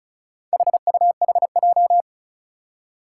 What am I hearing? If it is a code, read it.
Morse code, heard as a keyed tone, about 36 wpm, 690 Hz. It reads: HUHJ
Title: Four Challenging Letter-Number Combinations 35wpm